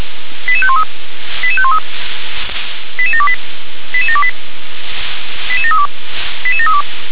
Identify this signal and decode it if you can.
Five-tone selective call (ZVEI), heard as quick runs of five tones, tones of about 70 ms. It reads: 88513 88513 88728 88728 88732 88732